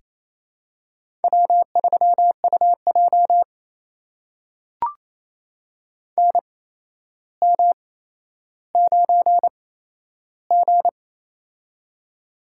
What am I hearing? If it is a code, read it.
Morse code, heard as a keyed tone, about 28 wpm, spaced as for 12 wpm, 700 Hz. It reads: W3UJ NM9G
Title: USA Call Signs - 4 Characters - 8x Character Spacing - 28wpm